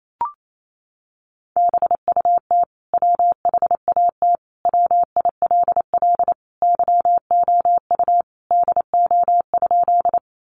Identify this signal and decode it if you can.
Morse code, heard as a keyed tone, about 28 wpm, 700 Hz. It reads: BUT WHAT WILL YOU DO?
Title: Sentences from Top 100 Words 28wpm